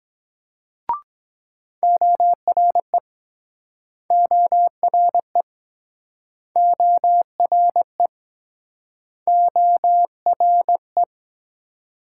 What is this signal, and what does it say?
Morse code, heard as a keyed tone, about 24 wpm, 700 Hz. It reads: ORE ORE ORE ORE